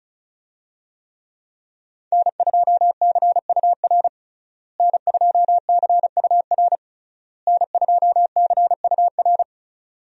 Morse code, heard as a keyed tone, about 34 wpm, 700 Hz. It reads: N2CUR N2CUR N2CUR
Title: USA Call Signs - 5 Characters - Repeated 3x - 35wpm